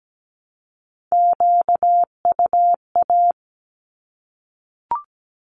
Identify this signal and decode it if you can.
Morse code, heard as a keyed tone, about 17 wpm, 700 Hz. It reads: QUA
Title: Q-Codes 17wpm